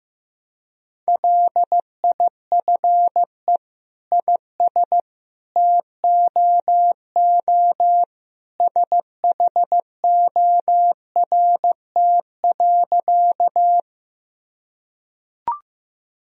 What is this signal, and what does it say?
Morse code, heard as a keyed tone, about 16 wpm, 700 Hz. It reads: LIFE IS TOO SHORT.
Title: Sentences from Top 1000 Words - Review Part 2 - 15wpm